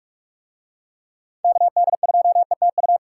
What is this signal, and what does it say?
Morse code, heard as a keyed tone, about 44 wpm, 700 Hz. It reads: KD2ETU